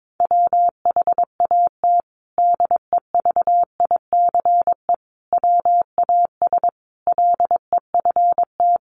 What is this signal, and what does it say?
Morse code, heard as a keyed tone, about 22 wpm, 700 Hz. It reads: WHAT DEVICE WAS LEFT
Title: Sentences from Top 1500 Words 22wpm